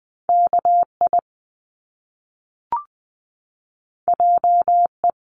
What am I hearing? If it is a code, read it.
Morse code, heard as a keyed tone, about 19 wpm, 700 Hz. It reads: KI JE
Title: Top Two Letter Combinations in Words 20wpm